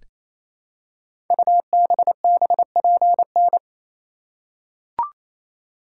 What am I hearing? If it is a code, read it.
Morse code, heard as a keyed tone, about 27 wpm, 700 Hz. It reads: UBBPN
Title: Code Groups - 5 Letter:Numbers - 28wpm